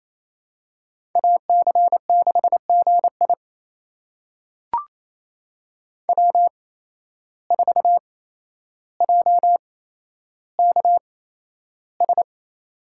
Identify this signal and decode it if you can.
Morse code, heard as a keyed tone, about 28 wpm, spaced as for 12 wpm, 700 Hz. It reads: AC6GI W4JKS
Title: USA Call Signs - 5 Characters - 8x Character Spacing - 28wpm